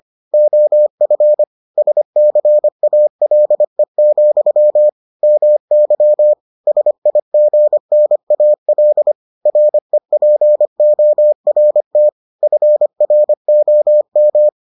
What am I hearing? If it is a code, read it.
Morse code, heard as a keyed tone, about 25 wpm, 600 Hz. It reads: OF SCALE, MY SIGNAL REPORT FROM